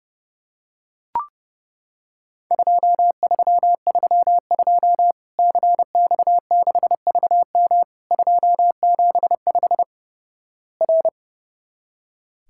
Morse code, heard as a keyed tone, about 30 wpm, 700 Hz. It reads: 2332 CX6VM 275 R
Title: Worked All Europe DX Contest - QTC Exchange - 30wpm